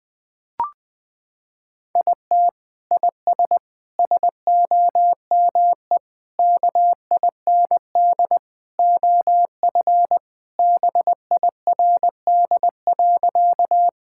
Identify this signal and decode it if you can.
Morse code, heard as a keyed tone, about 20 wpm, 700 Hz. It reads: IT IS SOME KIND OF BIRD.
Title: Sentences with 3-Word Phrases - 20wpm